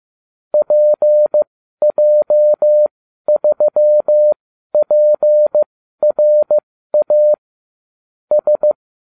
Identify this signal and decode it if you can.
Morse code, heard as a keyed tone, about 15 wpm, spaced as for 12 wpm, 600 Hz. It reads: PJ3PRA S